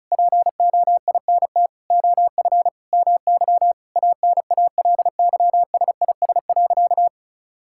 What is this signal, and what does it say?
Morse code, heard as a keyed tone, about 36 wpm, 700 Hz. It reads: POINT OF MY ANALYSIS.
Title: Sentences from Top 1000 Words - Review Part 5 - 35wpm